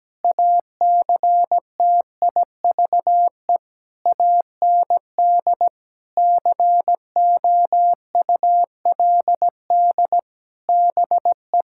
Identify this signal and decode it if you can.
Morse code, heard as a keyed tone, about 17 wpm, 700 Hz. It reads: ACTIVE AND COULD BE
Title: Sentences from Top 1000 Words - Review Part 6 - 17wpm